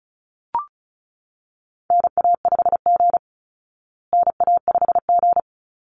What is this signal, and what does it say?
Morse code, heard as a keyed tone, about 34 wpm, 700 Hz. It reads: NA5G NA5G
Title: Call Signs Repeated 3x - 35wpm